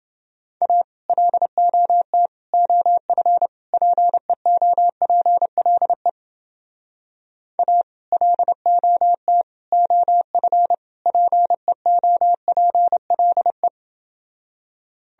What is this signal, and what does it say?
Morse code, heard as a keyed tone, about 30 wpm, 700 Hz. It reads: A LOT OF PEOPLE A LOT OF PEOPLE